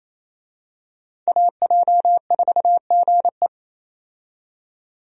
Morse code, heard as a keyed tone, about 28 wpm, 700 Hz. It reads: AJ4GE
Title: USA Call Signs - 5 Characters - 28wpm